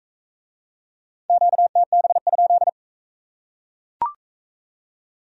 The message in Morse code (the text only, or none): QTB?